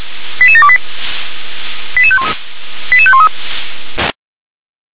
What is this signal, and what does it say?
Five-tone selective call (ZVEI), heard as quick runs of five tones, tones of about 70 ms. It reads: 88728 88513